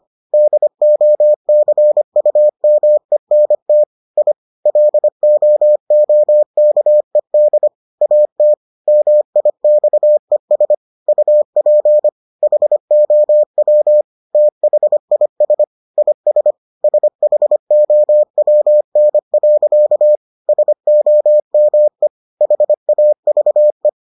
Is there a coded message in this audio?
Yes: Morse code from 0.3 to 23.9 s, heard as a keyed tone, about 25 wpm, 600 Hz. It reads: DOCUMENT I LOOKED AT MIXES UP HOW THIS IS SHOWN. SOME HAVE